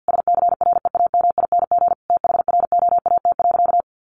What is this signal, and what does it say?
Morse code, heard as a keyed tone, about 50 wpm, 700 Hz. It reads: SCREAMING THROAT.
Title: A Sound Of Thunder 50wpm